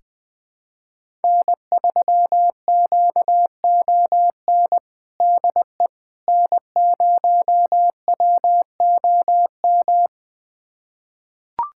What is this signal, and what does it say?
Morse code, heard as a keyed tone, about 21 wpm, 700 Hz. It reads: N3QON DE N0WOM
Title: USA Callsign DE Callsign - 5 Characters - 20wpm